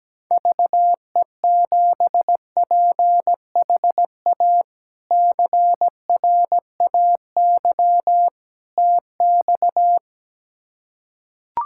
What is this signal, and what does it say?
Morse code, heard as a keyed tone, about 17 wpm, 700 Hz. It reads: VE7PHA CRAY TX
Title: K1USN SST QSO Practice - 17wpm